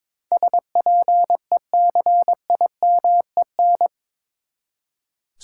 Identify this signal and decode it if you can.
Morse code, heard as a keyed tone, about 22 wpm, 700 Hz. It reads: SPECIMEN
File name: practice / 22wpm / Eight Letter Words - 22wpm